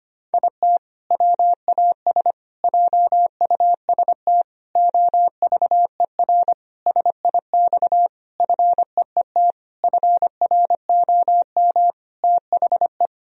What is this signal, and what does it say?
Morse code, heard as a keyed tone, about 25 wpm, 700 Hz. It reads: IT WAS JUST OVER SIX FEET FROM THE
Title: Sentences from Top 1000 Words - Review Part 2 - 25wpm